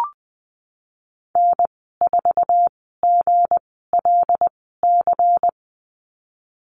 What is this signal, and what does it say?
Morse code, heard as a keyed tone, about 20 wpm, spaced as for 15 wpm, 700 Hz. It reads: N4GLC
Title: 173 Lesson 28g - Number 9 - Call Signs 20wpm